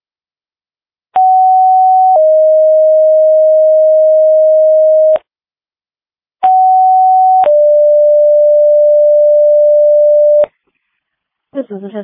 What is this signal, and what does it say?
Two-tone page, heard as a one-second tone, then a three-second tone, about 750 then 620 Hz, about 750 then 590 Hz, with speech. It is heard from 1.2 to 10.4 s.